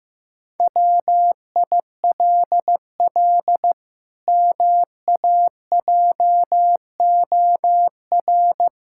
Morse code, heard as a keyed tone, about 14 wpm, 700 Hz. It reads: WILL MAJOR
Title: Pangram Challenge v2 - 15wpm